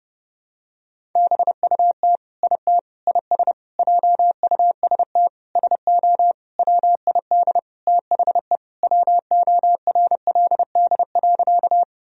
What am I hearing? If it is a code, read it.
Morse code, heard as a keyed tone, about 32 wpm, 700 Hz. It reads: BUT IT IS JUST SO WID THE WORLD.
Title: Sentences from Top 200 Words and Abbreviations 30wpm